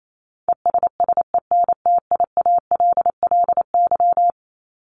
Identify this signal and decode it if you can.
Morse code, heard as a keyed tone, about 28 wpm, 700 Hz. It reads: ESSENTIALLY
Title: Top 1251-1500 Words 28wpm